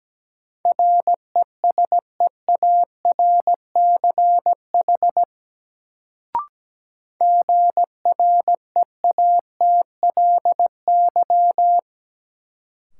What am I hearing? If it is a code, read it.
Morse code, heard as a keyed tone, about 17 wpm, 700 Hz. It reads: RESEARCH GREATLY
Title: Sentences from Top 1400 Words - New Word Drill Alt - 17wpm